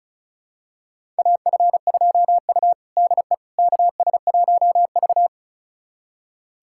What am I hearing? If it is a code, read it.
Morse code, heard as a keyed tone, about 34 wpm, 700 Hz. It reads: AF2U DE KS1V
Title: USA Callsign DE Callsign - 4 Characters - 35wpm